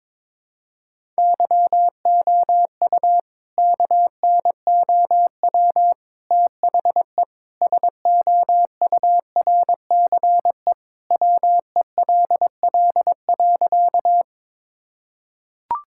Morse code, heard as a keyed tone, about 22 wpm, 700 Hz. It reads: YOU KNOW THE SOURCE WELL.
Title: Sentences from Top 1000 Words - Review Part 5 - 22wpm